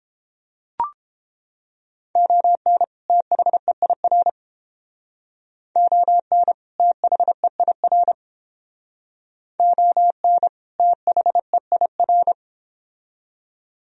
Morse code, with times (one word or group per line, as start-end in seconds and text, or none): 2.2-2.8 ON
3.1-4.3 THEIR
5.8-6.5 ON
6.8-8.1 THEIR
9.6-10.5 ON
10.8-12.3 THEIR